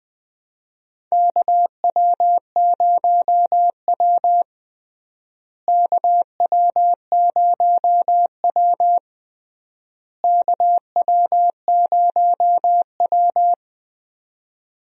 Morse code, heard as a keyed tone, about 20 wpm, 700 Hz. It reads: KW0W KW0W KW0W
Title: USA Call Signs - 4 Characters - Repeated 3x - 20wpm